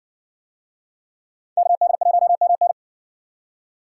Morse code, heard as a keyed tone, about 60 wpm, 700 Hz. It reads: KD1MN